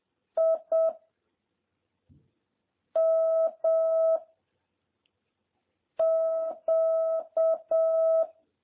Morse code, heard as a keyed tone, about 7 wpm, 650 Hz. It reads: I M Q